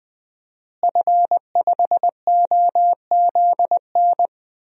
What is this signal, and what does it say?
Morse code, heard as a keyed tone, about 20 wpm, 700 Hz. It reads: F5OZN
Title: DXCC-100 Call Signs - 5 Characters - Repeated 3x - 20wpm